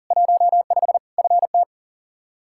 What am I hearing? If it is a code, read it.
Morse code, heard as a keyed tone, about 40 wpm, 700 Hz. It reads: 15 FT
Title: Practice QSO - Exchange 2 - Info - 40wpm